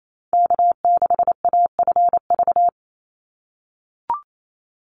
Morse code, heard as a keyed tone, about 28 wpm, 700 Hz. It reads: K6AFV
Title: USA Call Signs 28wpm